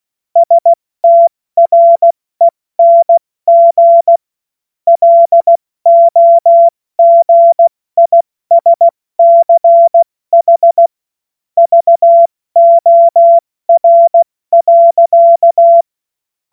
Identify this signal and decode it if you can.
Morse code, heard as a keyed tone, about 16 wpm, spaced as for 14 wpm, 680 Hz. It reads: STRENG LOGISCH VOR.
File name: Morsekurs / Witze / Witz 21. Juni 2022 70 F80